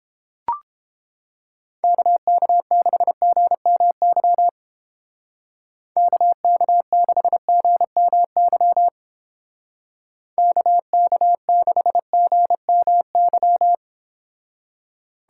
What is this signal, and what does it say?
Morse code, heard as a keyed tone, about 29 wpm, 700 Hz. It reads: KK6GMY KK6GMY KK6GMY